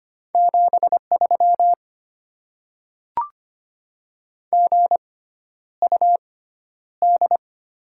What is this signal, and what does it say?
Morse code, heard as a keyed tone, about 25 wpm, spaced as for 11 wpm, 690 Hz. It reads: 73 GUD
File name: Top 20 QSO Elements - 6x Character Spacing - 25wpm